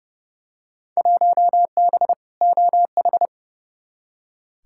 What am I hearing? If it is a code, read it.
Morse code, heard as a keyed tone, about 30 wpm, 700 Hz. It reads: 1B OH
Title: Field Day Exchanges - 30wpm